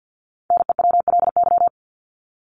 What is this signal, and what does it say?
Morse code, heard as a keyed tone, about 48 wpm, 700 Hz. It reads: NEWLY
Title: LY Suffix - Top 100 Words with Base Words 50wpm